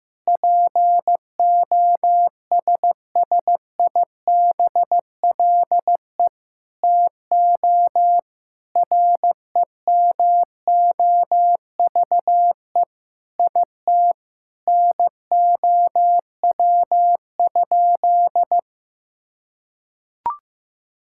Morse code, heard as a keyed tone, about 15 wpm, 700 Hz. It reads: POSSIBLE TO REMOVE IT NOW?
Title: Sentences from Top 1000 Words - Review Part 5 - 15wpm